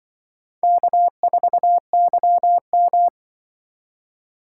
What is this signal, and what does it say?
Morse code, heard as a keyed tone, about 24 wpm, 700 Hz. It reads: K4YM
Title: Speed Racing - USA 1x2 Call Signs 20wpm